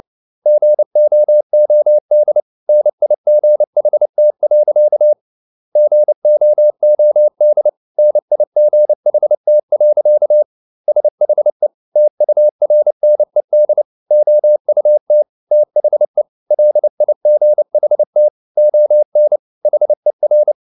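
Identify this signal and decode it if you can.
Morse code, heard as a keyed tone, about 29 wpm, 600 Hz. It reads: GOOD NIGHT. GOOD NIGHT. SHE TURNED OUT THE LIGHT ON HER
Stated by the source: Morse code at 29 wpm, sending Ken Follett's novel Never in English, from CW med Gnister.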